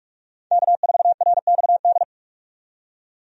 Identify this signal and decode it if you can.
Morse code, heard as a keyed tone, about 45 wpm, 700 Hz. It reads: K4RXD
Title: USA Call Signs - 5 Characters - 45wpm